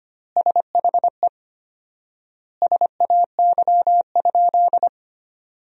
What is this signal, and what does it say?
Morse code, heard as a keyed tone, about 25 wpm, 700 Hz. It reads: SHE SAY?
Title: 4x Word-Spacing - Sentences from Top 200 Words - 25wpm